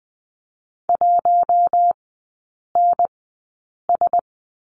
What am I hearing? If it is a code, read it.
Morse code, heard as a keyed tone, about 19 wpm, spaced as for 8 wpm, 700 Hz. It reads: 1NS
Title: 074 Lesson 13e - Letter L - Three Characters 20wpm